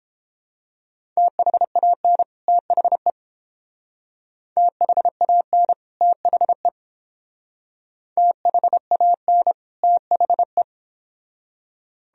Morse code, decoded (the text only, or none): THAN THE THAN THE THAN THE